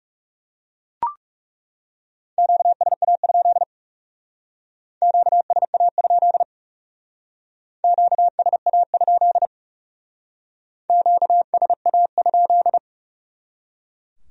Morse code, heard as a keyed tone, about 38 wpm, 700 Hz. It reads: QSA? QSA? QSA? QSA?